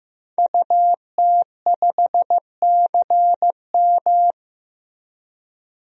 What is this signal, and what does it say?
Morse code, heard as a keyed tone, about 15 wpm, 700 Hz. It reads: UT5CM
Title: DXCC-100 Call Signs - 5 Characters - 15wpm